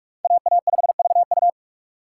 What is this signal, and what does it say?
Morse code, heard as a keyed tone, about 46 wpm, 700 Hz. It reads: AA5VU
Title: K1USN SST QSO Practice - 45wpm